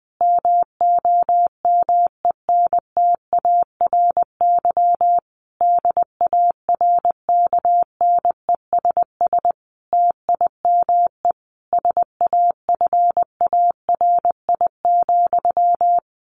Morse code, heard as a keyed tone, about 20 wpm, 700 Hz. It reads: MOMENTARY DARKNESS TIME SAFARI,
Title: A Sound Of Thunder 20wpm